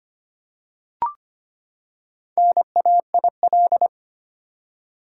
Morse code, heard as a keyed tone, about 25 wpm, 700 Hz. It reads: NAIL
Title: Four Letter Words - 25wpm